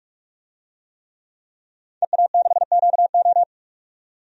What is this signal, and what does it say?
Morse code, heard as a keyed tone, about 45 wpm, 700 Hz. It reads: EA6QO
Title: DXCC-100 Call Signs - 5 Characters - 45wpm